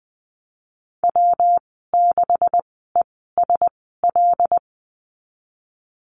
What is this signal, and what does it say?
Morse code, heard as a keyed tone, about 20 wpm, spaced as for 15 wpm, 700 Hz. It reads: W6ESL
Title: 128 Lesson 21g - Number 6 - Call Signs 20wpm